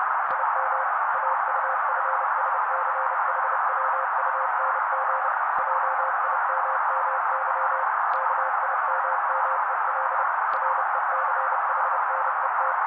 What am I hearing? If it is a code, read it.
Morse code, heard as a keyed tone, about 30 wpm, 530 Hz. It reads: IM AUFSCHWUNG JIMMY KIMMEL RECHNET
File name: daily / 29wpm noise